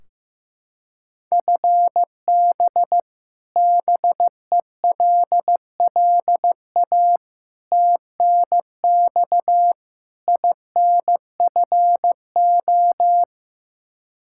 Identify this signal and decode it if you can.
Morse code, heard as a keyed tone, about 15 wpm, 700 Hz. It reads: FB BELLA TNX INFO
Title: Practice QSO - Exchange 3 - More Info - 15wpm